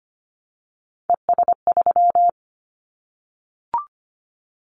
Morse code, heard as a keyed tone, about 25 wpm, 700 Hz. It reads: ES3
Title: Callsign Prefixes for Top 50 DXCC - 25wpm